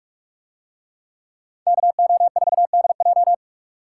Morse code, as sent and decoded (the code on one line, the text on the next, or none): -.- --- ....- -.. .---
KO4DJ